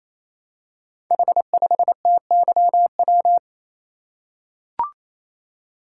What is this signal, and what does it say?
Morse code, heard as a keyed tone, about 27 wpm, 700 Hz. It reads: H5TYW
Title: Code Groups - 5 Letter:Numbers - 28wpm